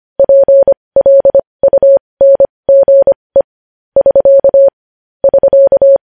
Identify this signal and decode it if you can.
Morse code, heard as a keyed tone, about 25 wpm, spaced as for 20 wpm, 550 Hz. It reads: PLUNGE <SK> <SK>
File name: podcast / News headlines-1x-25x20